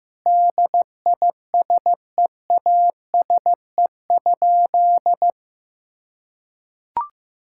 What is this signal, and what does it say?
Morse code, heard as a keyed tone, about 15 wpm, 700 Hz. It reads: DISEASE?
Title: Sentences from Top 1000 Words - Review Part 6 - 15wpm